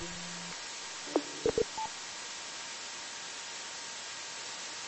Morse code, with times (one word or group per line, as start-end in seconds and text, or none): none